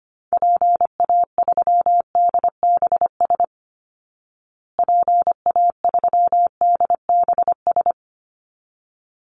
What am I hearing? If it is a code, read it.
Morse code, heard as a keyed tone, about 25 wpm, 700 Hz. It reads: PA3DBS PA3DBS
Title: Call Signs Repeated 3x - 25wpm